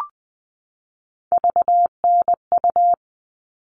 Morse code, heard as a keyed tone, about 20 wpm, 700 Hz. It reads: VNU
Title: All Cut Numbers - 3-Digit 20wpm